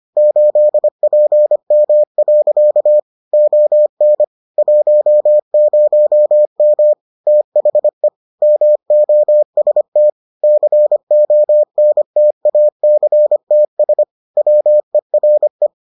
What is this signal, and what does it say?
Morse code, heard as a keyed tone, about 25 wpm, 600 Hz. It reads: 8PM. ON 10M THE MOST CONTACTS WERE